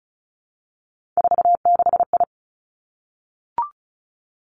Morse code, heard as a keyed tone, about 36 wpm, 700 Hz. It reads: 46I
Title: Three Challenging Letter-Number Combinations 35wpm